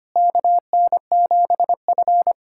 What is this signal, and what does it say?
Morse code, heard as a keyed tone, about 25 wpm, 700 Hz. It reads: KN7F
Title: USA Callsign DE Callsign - 4 Characters - 25wpm